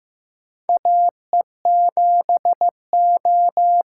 Morse code, heard as a keyed tone, about 15 wpm, 700 Hz. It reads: AE7O
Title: USA Call Signs - 5 Characters - 6x Character Spacing - 15wpm